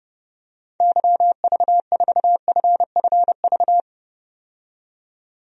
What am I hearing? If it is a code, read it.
Morse code, heard as a keyed tone, about 32 wpm, 700 Hz. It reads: YV4FFV
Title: DXCC-100 Call Signs - 6 Characters - 30wpm